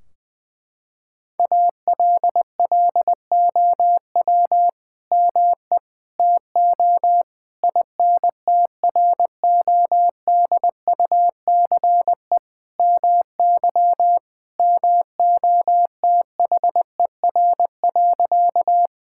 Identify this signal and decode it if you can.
Morse code, heard as a keyed tone, about 20 wpm, 700 Hz. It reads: ALLOW ME TO INTRODUCE MY MOTHER.
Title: Sentences from Top 1600 Words 20wpm